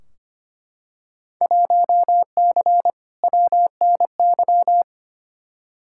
Morse code, heard as a keyed tone, about 25 wpm, 700 Hz. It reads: 1C WNY
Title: Field Day Exchanges Encore - 25wpm